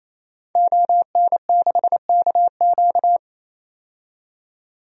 Morse code, heard as a keyed tone, about 27 wpm, 700 Hz. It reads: ON6KQ
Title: DXCC-100 Call Signs - 5 Characters - 28wpm